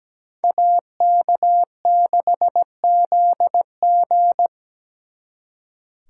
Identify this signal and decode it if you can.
Morse code, heard as a keyed tone, about 20 wpm, 700 Hz. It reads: AK6ZG
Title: Speed Racing - USA 2x2 Call Signs 17wpm